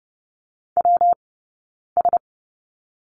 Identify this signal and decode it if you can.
Morse code, heard as a keyed tone, about 30 wpm, 700 Hz. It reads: W S
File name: Lesson 20d - Number 3 - Two Characters 30wpm